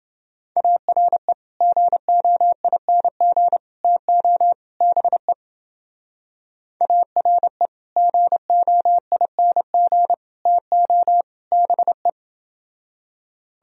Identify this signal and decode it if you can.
Morse code, heard as a keyed tone, about 28 wpm, 700 Hz. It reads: ARE GOING TO BE ARE GOING TO BE